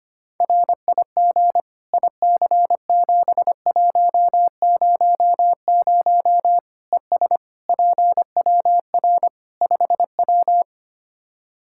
Morse code, heard as a keyed tone, about 25 wpm, 700 Hz. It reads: RIG IC7100 ES PWR 5W
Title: Practice QSO - Exchange 2 - Info - 25wpm